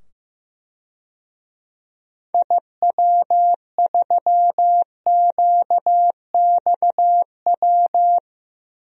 Morse code, heard as a keyed tone, about 15 wpm, 700 Hz. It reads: IW3QXW
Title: DXCC-100 Call Signs - 6 Characters - 15wpm